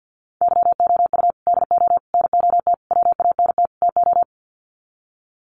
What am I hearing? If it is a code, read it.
Morse code, heard as a keyed tone, about 48 wpm, 700 Hz. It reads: YOU DO NOT WANT TO
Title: Sets of 5 Words 50wpm